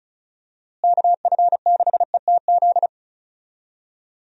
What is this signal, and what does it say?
Morse code, heard as a keyed tone, about 34 wpm, 700 Hz. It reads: KF6ETZ
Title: USA Call Signs - 6 Characters - Repeated 3x - 35wpm